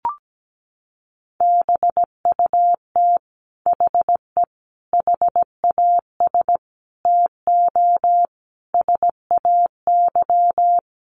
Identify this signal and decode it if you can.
Morse code, heard as a keyed tone, about 17 wpm, 700 Hz. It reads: BUT HE HAS TO SAY
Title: Sentences from Top 600 Words 17wpm